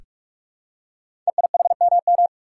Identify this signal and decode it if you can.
Morse code, heard as a keyed tone, about 44 wpm, 700 Hz. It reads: EIHMM
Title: Code Groups - 5 Letters - 45wpm